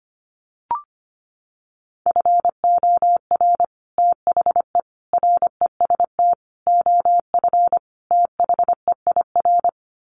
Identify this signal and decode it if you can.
Morse code, heard as a keyed tone, about 25 wpm, 700 Hz. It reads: FOR THE REST OF THEIR
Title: Sets of 5 Words Encore2 25wpm